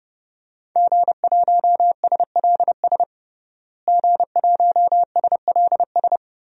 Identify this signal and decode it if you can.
Morse code, heard as a keyed tone, about 30 wpm, 700 Hz. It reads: G1SLS G1SLS
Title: DXCC-100 Call Signs - 5 Characters - Repeated 2x - 30wpm